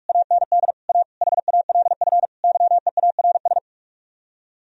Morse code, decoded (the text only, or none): AND A HALF YEARS